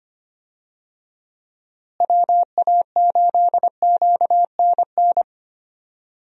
Morse code, heard as a keyed tone, about 25 wpm, 700 Hz. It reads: WA8QNN
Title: USA Call Signs - 6 Characters - Repeated 3x - 25wpm